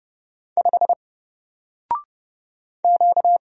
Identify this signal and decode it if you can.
Morse code, heard as a keyed tone, about 30 wpm, 700 Hz. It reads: 5 Q
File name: Single-Letter-Number-30wpm